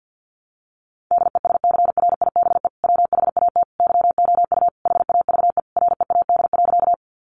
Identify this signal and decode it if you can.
Morse code, heard as a keyed tone, about 50 wpm, 700 Hz. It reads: DESCRIBE WHAT YOU HAVE READ.
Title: Sentences from Top 1000 Words 50wpm